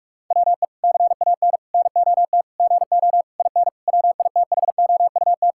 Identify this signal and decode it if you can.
Morse code, heard as a keyed tone, about 41 wpm, 700 Hz. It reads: WE CAN NOT GO IN WITHOUT
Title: Sentences from Top 1000 Words - Review Part 6 - 45wpm